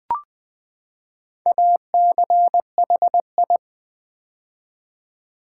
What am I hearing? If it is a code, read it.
Morse code, heard as a keyed tone, about 20 wpm, 700 Hz. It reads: ACHI
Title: Build A Word - 4 Letters At A Time - 11-Letter Words - 20wpm